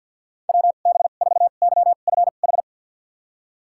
Morse code, heard as a keyed tone, about 50 wpm, spaced as for 36 wpm, 700 Hz. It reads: WB4YFH
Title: USA Call Signs - 2x Character Spacing - 50wpm